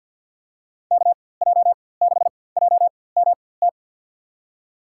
Morse code, heard as a keyed tone, about 52 wpm, spaced as for 23 wpm, 700 Hz. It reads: KJ6JMT
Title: USA Call Signs - 6 Characters - 4x Character Spacing - 50wpm